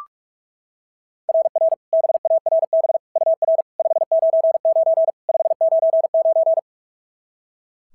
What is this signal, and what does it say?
Morse code, heard as a keyed tone, about 44 wpm, 690 Hz. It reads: RR BARB UR 599 599
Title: SOTA QSO Practice 45wpm